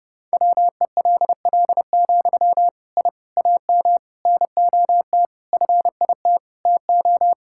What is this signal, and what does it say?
Morse code, heard as a keyed tone, about 30 wpm, 700 Hz. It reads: WELL, I AM NOT FIT TO
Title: Sentences from Top 1000 Words - Review Part 5 - 30wpm